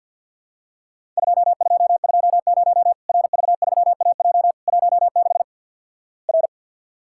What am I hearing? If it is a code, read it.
Morse code, heard as a keyed tone, about 51 wpm, 700 Hz. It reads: 2220 RV3AJ 16 R
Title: Worked All Europe DX Contest - QTC Exchange with Repeats - 50wpm